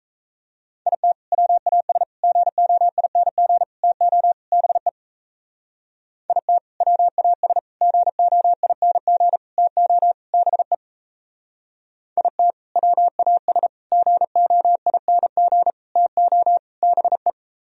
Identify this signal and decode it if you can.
Morse code, heard as a keyed tone, about 36 wpm, 700 Hz. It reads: IT WAS GOING TO BE IT WAS GOING TO BE IT WAS GOING TO BE